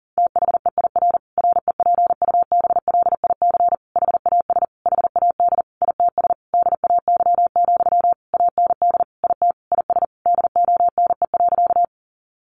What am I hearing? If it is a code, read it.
Morse code, heard as a keyed tone, about 40 wpm, 700 Hz. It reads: THEIR REPUBLIC HAS HAD ITS DAY, AND IT IS DONE.
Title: Sentences from Top 1500 Words 40wpm